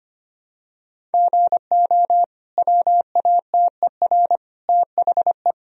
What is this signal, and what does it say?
Morse code, heard as a keyed tone, about 25 wpm, 700 Hz. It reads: GO WATER THE
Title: Sentences from Top 1000 Words - Review Part 1 - 25wpm